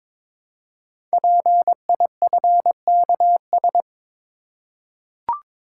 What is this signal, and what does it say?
Morse code, heard as a keyed tone, about 22 wpm, 700 Hz. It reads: PIFKS
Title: Code Groups - 5 Letters - 22wpm